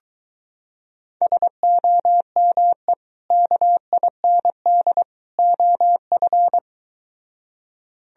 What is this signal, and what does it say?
Morse code, heard as a keyed tone, about 23 wpm, 700 Hz. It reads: SOME KIND OF